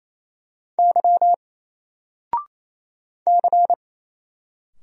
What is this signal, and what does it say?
Morse code, heard as a keyed tone, about 27 wpm, 700 Hz. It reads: Y C
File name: Single-Letter-Number-28wpm